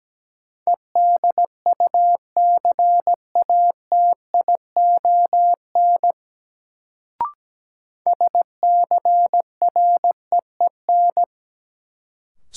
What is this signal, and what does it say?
Morse code, heard as a keyed tone, about 17 wpm, 700 Hz. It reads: EDUCATION SCREEN
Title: Sentences from Top 1000 Words - Review Part 6 - New Word Drill - 17wpm